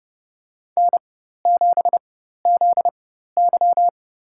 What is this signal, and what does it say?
Morse code, heard as a keyed tone, about 30 wpm, spaced as for 14 wpm, 700 Hz. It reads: N7ZY
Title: USA Call Signs - 4 Characters - 4x Character Spacing - 30wpm